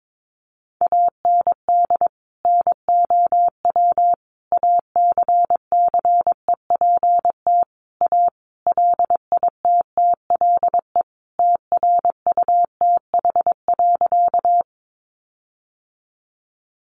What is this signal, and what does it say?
Morse code, heard as a keyed tone, about 22 wpm, 700 Hz. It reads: AND NOW ACCEPT A LITTLE TRUTH.
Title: Sentences from Top 1300 Words 22wpm